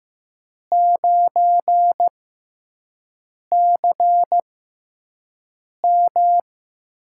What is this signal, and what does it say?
Morse code, heard as a keyed tone, about 15 wpm, spaced as for 5 wpm, 700 Hz. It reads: 9CM
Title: USA Call Signs - 5 Characters - 6x Character Spacing - 15wpm